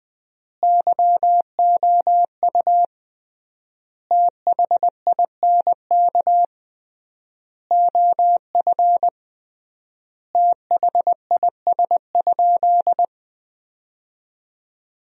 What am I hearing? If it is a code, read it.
Morse code, heard as a keyed tone, about 21 wpm, 700 Hz. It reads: YOU THINK OF THIS?
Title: 3x Word-Spacing - Sentences from Top 200 Words - 20wpm